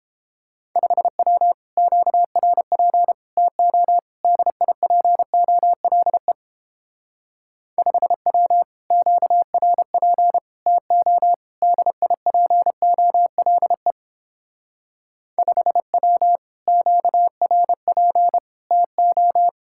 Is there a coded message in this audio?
Yes: Morse code from 0.8 to 19.5 s, heard as a keyed tone, about 29 wpm, 700 Hz. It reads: 5W QRP TO DIPOLE 5W QRP TO DIPOLE 5W QRP TO